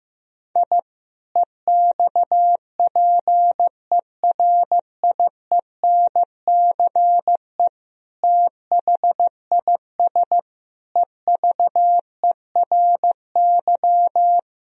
Morse code, heard as a keyed tone, about 15 wpm, 700 Hz. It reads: I EXPERIENCE THIS EVERY
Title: Sentences from Top 1000 Words - Review Part 3 - 15wpm